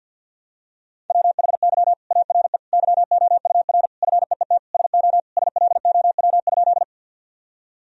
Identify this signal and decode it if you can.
Morse code, heard as a keyed tone, about 50 wpm, 700 Hz. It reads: WHY ARE YOUR FEET SO SLOW?
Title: Sentences from Top 1000 Words - Review Part 2 - 50wpm